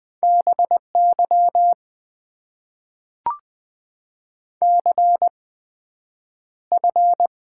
Morse code, heard as a keyed tone, about 20 wpm, spaced as for 9 wpm, 700 Hz. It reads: BY CF
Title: Top 90 QSO Elements - 8x Character Spacing - 20wpm